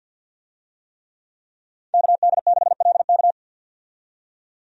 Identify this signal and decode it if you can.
Morse code, heard as a keyed tone, about 51 wpm, 700 Hz. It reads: KD6LK
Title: USA Call Signs - 5 Characters - 50wpm